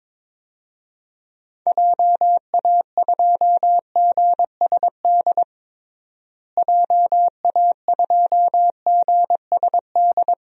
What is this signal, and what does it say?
Morse code, heard as a keyed tone, about 22 wpm, 700 Hz. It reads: JA2GSD JA2GSD
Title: DXCC-100 Call Signs - 6 Characters - Repeated 2x - 22wpm